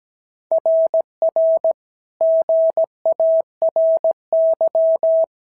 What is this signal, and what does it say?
Morse code, heard as a keyed tone, about 16 wpm, 650 Hz. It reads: RR GARY
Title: SOTA QSO Practice 17wpm